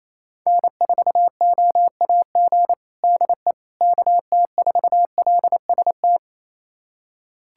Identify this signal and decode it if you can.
Morse code, heard as a keyed tone, about 28 wpm, 700 Hz. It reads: N4OAG DE KT4LST
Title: USA Callsign DE Callsign 28wpm